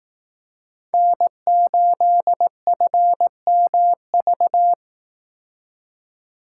Morse code, heard as a keyed tone, about 18 wpm, 700 Hz. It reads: N8FMV
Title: Speed Racing - USA 1x3 Call Signs 15wpm